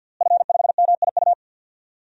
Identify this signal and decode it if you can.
Morse code, heard as a keyed tone, about 51 wpm, 700 Hz. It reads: F5MIU
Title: DXCC-100 Call Signs - 5 Characters - Repeated 2x - 50wpm